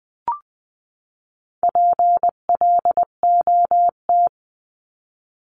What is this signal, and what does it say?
Morse code, heard as a keyed tone, about 20 wpm, 700 Hz. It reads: PLOT
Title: 259 Lesson 42d - Words 18-20wpm